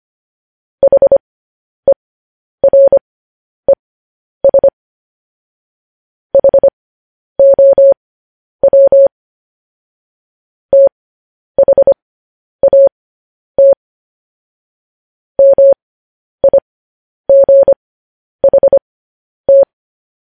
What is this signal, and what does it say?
Morse code, heard as a keyed tone, about 25 wpm, spaced as for 10 wpm, 550 Hz. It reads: HERES HOW THAT MIGHT
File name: podcast / News headlines-1x-25x10